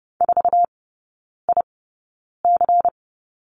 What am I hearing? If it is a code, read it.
Morse code, heard as a keyed tone, about 31 wpm, spaced as for 9 wpm, 690 Hz. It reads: 4IC
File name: Lesson 16e - Letter c - Three Characters 30wpm